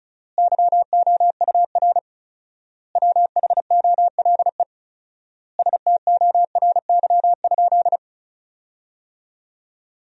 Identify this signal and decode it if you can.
Morse code, heard as a keyed tone, about 36 wpm, 700 Hz. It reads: YOUR WHOLE STORY?
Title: 4x Word-Spacing - Sentences from Top 400 Words - 35wpm